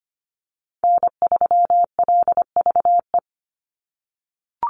Morse code, heard as a keyed tone, about 25 wpm, 700 Hz. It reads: N3LVE
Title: USA Call Signs 25wpm